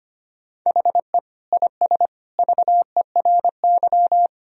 Morse code, heard as a keyed tone, about 25 wpm, 700 Hz. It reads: HE IS VERY
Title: Sentences from Top 1000 Words - Review Part 1 - 25wpm